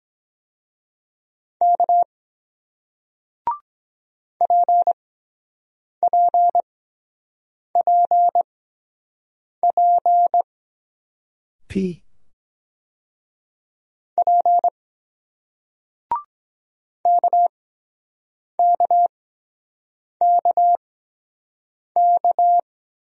Morse code, heard as a keyed tone, about 23 wpm, spaced as for 7 wpm, 700 Hz. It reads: K PPPP P KKKK